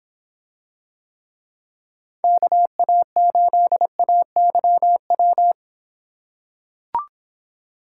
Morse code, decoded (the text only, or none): KA8AYW